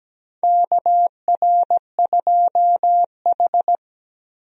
Morse code, heard as a keyed tone, about 17 wpm, 700 Hz. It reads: KR2H
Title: USA Call Signs - 4 Characters - 17wpm